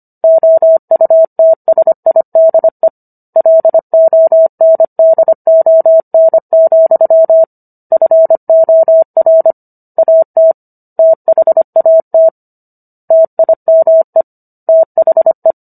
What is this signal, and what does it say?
Morse code, heard as a keyed tone, about 25 wpm, 650 Hz. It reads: OUTSIDE LONDON, FOR AT THAT TIME THE